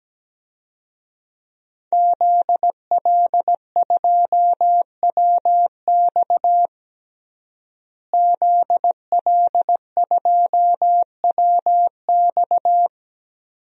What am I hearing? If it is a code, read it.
Morse code, heard as a keyed tone, about 17 wpm, 700 Hz. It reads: ZL2WX ZL2WX
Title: DXCC-100 Call Signs - 5 Characters - Repeated 2x - 17wpm